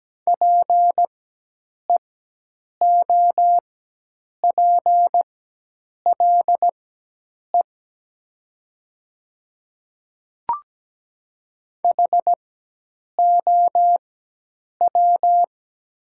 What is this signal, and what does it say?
Morse code, heard as a keyed tone, about 17 wpm, spaced as for 8 wpm, 700 Hz. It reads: PEOPLE HOW
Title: 4x Character Spacing - Top 100 Words - 17wpm